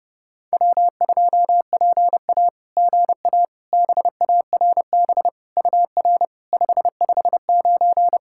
Morse code, heard as a keyed tone, about 30 wpm, 700 Hz. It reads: W2PA GA BARB UR 559
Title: SOTA QSO Practice 30wpm